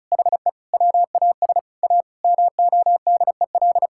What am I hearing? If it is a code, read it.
Morse code, heard as a keyed tone, about 36 wpm, 700 Hz. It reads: HE WAS A MODEL
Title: Sentences from Top 1000 Words - Review Part 5 - 35wpm